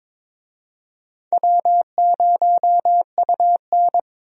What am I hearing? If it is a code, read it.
Morse code, heard as a keyed tone, about 21 wpm, 700 Hz. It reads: W0UN
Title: USA Call Signs - 4 Characters - 22wpm